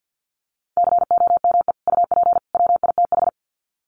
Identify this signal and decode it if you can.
Morse code, heard as a keyed tone, about 50 wpm, 700 Hz. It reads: COME UP WITH
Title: Sets of 3 Words 50wpm